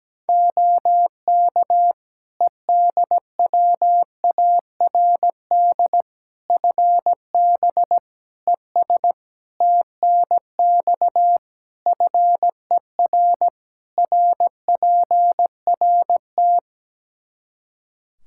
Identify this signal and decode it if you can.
Morse code, heard as a keyed tone, about 17 wpm, 700 Hz. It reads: OK EDWARD FB ES TNX FER RPRT
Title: Practice QSO - Exchange 2 - Info - 17wpm